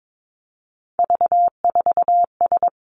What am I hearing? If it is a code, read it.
Morse code, heard as a keyed tone, about 22 wpm, 700 Hz. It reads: V4S
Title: Three Challenging Letter-Number Combinations 22wpm